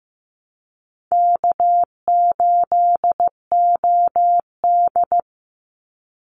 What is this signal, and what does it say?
Morse code, heard as a keyed tone, about 15 wpm, 700 Hz. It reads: K8OD
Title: Easy Call Signs Repeated 3x - 15wpm